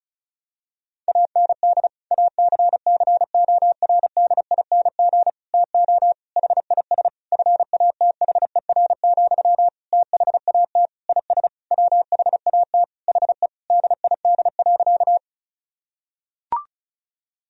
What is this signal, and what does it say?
Morse code, heard as a keyed tone, about 34 wpm, 700 Hz. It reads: AND ACCORDING TO HIS FATHER, THAT IS WHAT HE DID.
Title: Sentences from Top 1000 Words - Review Part 4 - 35wpm